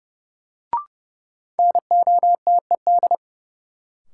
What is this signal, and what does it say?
Morse code, heard as a keyed tone, about 31 wpm, 700 Hz. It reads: NOTED
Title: Sets of 5 Words - New Word Drill - 30wpm